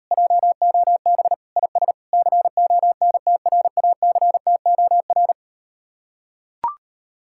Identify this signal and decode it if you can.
Morse code, heard as a keyed tone, about 39 wpm, 700 Hz. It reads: JOB IS CONTRACTOR